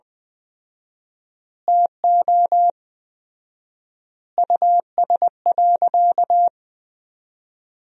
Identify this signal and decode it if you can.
Morse code, heard as a keyed tone, about 20 wpm, 700 Hz. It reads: TO US.
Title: 4x Word-Spacing - Sentences from Top 500 Words - 20wpm